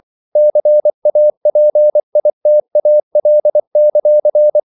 Morse code, heard as a keyed tone, about 24 wpm, 600 Hz. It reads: CAPITAL;
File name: CW / Practice / NationalCapitals24